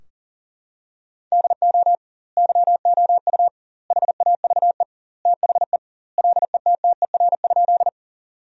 Morse code, heard as a keyed tone, about 40 wpm, 700 Hz. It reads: DO YOU HAVE THE LETTER?
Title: 2x Word-Spacing - Sentences from Top 300 Words - 40wpm